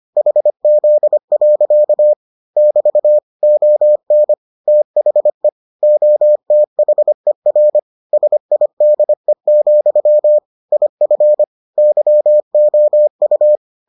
Morse code, heard as a keyed tone, about 25 wpm, 600 Hz. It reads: HZ. <BT> ON THE OTHER SIDE, IF YOU